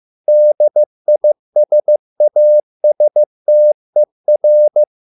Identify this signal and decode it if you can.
Morse code, heard as a keyed tone, about 15 wpm, 600 Hz. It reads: DISASTER